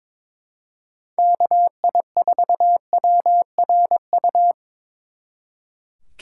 Morse code, heard as a keyed tone, about 25 wpm, 700 Hz. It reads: KI4WRU
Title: Speed Racing - USA 2x3 Call Signs 22wpm